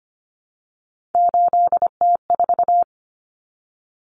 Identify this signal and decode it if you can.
Morse code, heard as a keyed tone, about 25 wpm, 700 Hz. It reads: 8T4
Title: Common Cut Numbers - 3-Digit 25wpm